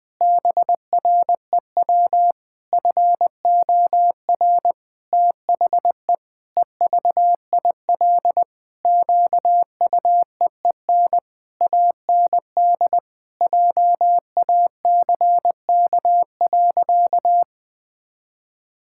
Morse code, heard as a keyed tone, about 20 wpm, 700 Hz. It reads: BREW FOR THE EVIL QUEEN AND JACK.
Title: Pangram Challenge v2 - 20wpm